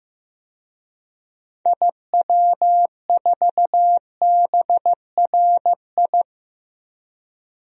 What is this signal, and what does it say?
Morse code, heard as a keyed tone, about 15 wpm, 700 Hz. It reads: IW4BRI
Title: DXCC-100 Call Signs - 6 Characters - 15wpm